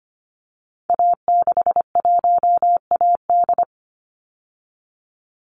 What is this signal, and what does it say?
Morse code, heard as a keyed tone, about 25 wpm, 700 Hz. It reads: A61AD
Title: Call Signs Repeated 3x - 25wpm